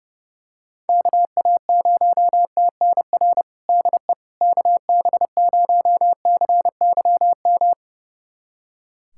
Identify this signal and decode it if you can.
Morse code, heard as a keyed tone, about 30 wpm, 700 Hz. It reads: KA0TNR DE KB0CYM
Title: USA Callsign DE Callsign 30wpm